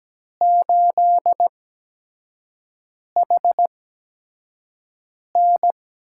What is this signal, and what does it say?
Morse code, heard as a keyed tone, about 17 wpm, spaced as for 5 wpm, 700 Hz. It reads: 8HN
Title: USA Call Signs - 5 Characters - 8x Character Spacing - 17wpm